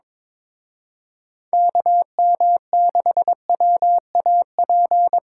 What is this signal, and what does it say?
Morse code, heard as a keyed tone, about 25 wpm, 700 Hz. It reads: KM6WAP
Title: Speed Racing - USA Call Signs 22wpm